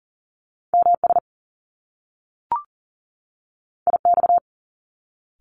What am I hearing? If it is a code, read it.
Morse code, heard as a keyed tone, about 40 wpm, 700 Hz. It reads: MS IX
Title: Top Two Letter Combinations in Words 40wpm